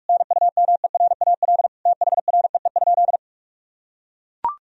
Morse code, heard as a keyed tone, about 44 wpm, 700 Hz. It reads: NUMERAL THREE?
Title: Sentences from Top 1000 Words - Review Part 2 - 45wpm